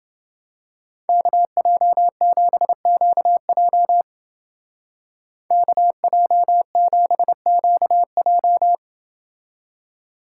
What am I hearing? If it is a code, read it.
Morse code, heard as a keyed tone, about 28 wpm, 700 Hz. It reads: KJ7QJ KJ7QJ